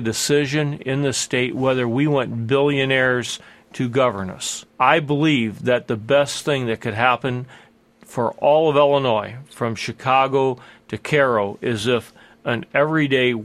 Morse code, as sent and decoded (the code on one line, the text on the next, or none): none